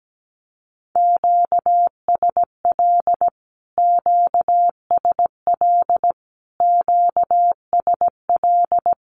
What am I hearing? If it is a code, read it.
Morse code, heard as a keyed tone, about 17 wpm, 700 Hz. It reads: QSL QSL QSL
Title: Q-Codes Repeated 3x 17wpm